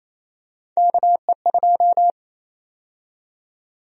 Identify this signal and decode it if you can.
Morse code, heard as a keyed tone, about 28 wpm, 700 Hz. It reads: KE2
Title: USA Call Sign Prefixes - Repeated 2x - 28wpm